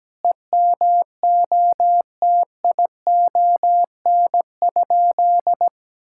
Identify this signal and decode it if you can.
Morse code, heard as a keyed tone, about 17 wpm, 700 Hz. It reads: EMOTION?
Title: Sentences from Top 1600 Words 17wpm